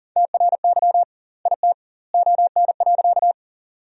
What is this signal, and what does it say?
Morse code, heard as a keyed tone, about 40 wpm, 700 Hz. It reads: TRY IT ON.
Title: 2x Word-Spacing - Sentences from Top 200 Words - 40wpm